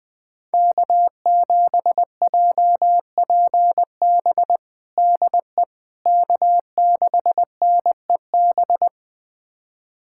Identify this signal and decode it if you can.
Morse code, heard as a keyed tone, about 21 wpm, 700 Hz. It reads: K7JPB DE K6NEB
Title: USA Callsign DE Callsign 20wpm